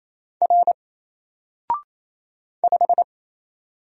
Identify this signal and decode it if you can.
Morse code, heard as a keyed tone, about 27 wpm, 700 Hz. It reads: R 5
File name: Single-Letter-Number-28wpm